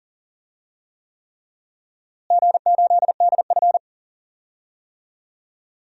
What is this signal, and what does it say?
Morse code, heard as a keyed tone, about 40 wpm, 700 Hz. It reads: G8DF
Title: DXCC-100 Call Signs - 4 Characters - Repeated 3x - 40wpm